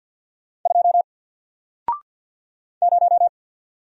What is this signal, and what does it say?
Morse code, heard as a keyed tone, about 48 wpm, 700 Hz. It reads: 2 0
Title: Single Letter-Number Alt 50wpm